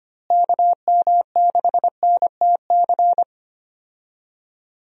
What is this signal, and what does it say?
Morse code, heard as a keyed tone, about 25 wpm, 700 Hz. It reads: KM6NTC
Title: USA Call Signs - 6 Characters - Repeated 3x - 25wpm